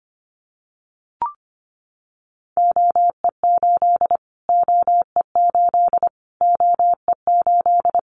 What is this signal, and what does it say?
Morse code, heard as a keyed tone, about 25 wpm, 700 Hz. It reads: OE8 OE8 OE8
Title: Callsign Prefixes for Top 25 DXCC - Repeated 3x - 25wpm